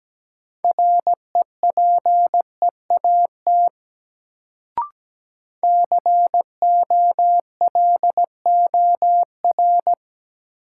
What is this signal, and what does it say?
Morse code, heard as a keyed tone, about 17 wpm, 700 Hz. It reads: REPEAT COLOR
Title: Sentences from Top 1600 Words - New Word Drill Alt - 17wpm